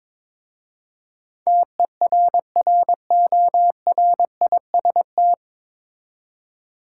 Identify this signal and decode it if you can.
Morse code, heard as a keyed tone, about 22 wpm, 700 Hz. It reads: TERRORIST
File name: 9 Letter Words - First 3 Letters Given - 22wpm